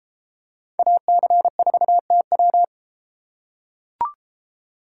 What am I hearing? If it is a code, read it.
Morse code, heard as a keyed tone, about 33 wpm, 700 Hz. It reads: AC4TW